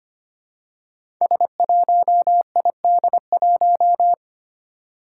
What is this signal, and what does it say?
Morse code, heard as a keyed tone, about 25 wpm, 700 Hz. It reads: S1ID1
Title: Five Challenging Letter-Number Combinations 25wpm